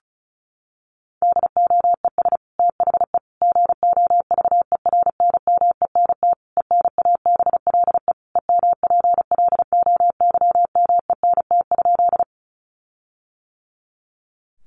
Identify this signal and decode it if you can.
Morse code, heard as a keyed tone, about 34 wpm, 700 Hz. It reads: DOES THE GOVERNMENT ENABLE EMPLOYMENT?
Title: Sentences from Top 1500 Words 35wpm